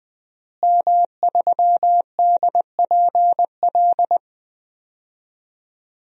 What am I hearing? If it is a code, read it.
Morse code, heard as a keyed tone, about 21 wpm, 700 Hz. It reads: M3DPL
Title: DXCC-100 Call Signs - 5 Characters - Repeated 3x - 20wpm